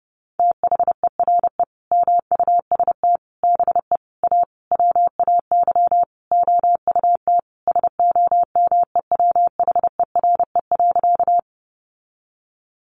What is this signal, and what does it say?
Morse code, heard as a keyed tone, about 30 wpm, 700 Hz. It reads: THERE MUST BE A WAY OUT SOMEWHERE.
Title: Sentences from Top 1400 Words 30wpm